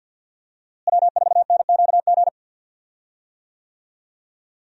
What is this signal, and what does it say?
Morse code, heard as a keyed tone, about 52 wpm, 700 Hz. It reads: W4NQG
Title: USA Call Signs - 5 Characters - Repeated 2x - 50wpm